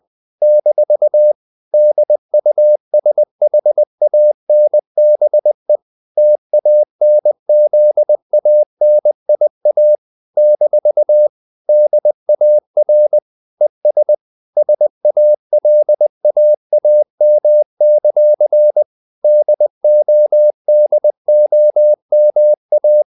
Morse code, heard as a keyed tone, about 20 wpm, 600 Hz. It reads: - DUSHANBE TANZANIA - DAR ES SALAAM; DODOMA
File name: CW / Practice / NationalCapitals20